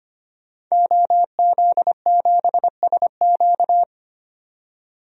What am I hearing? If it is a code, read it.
Morse code, heard as a keyed tone, about 25 wpm, 700 Hz. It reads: OZ7SQ
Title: DXCC-100 Call Signs - 5 Characters - 25wpm